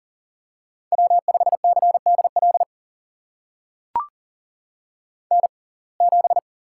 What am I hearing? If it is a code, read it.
Morse code, heard as a keyed tone, about 40 wpm, spaced as for 29 wpm, 700 Hz. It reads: W5CDL N7
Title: USA Call Signs - 5 Characters - 6x Character Spacing - 40wpm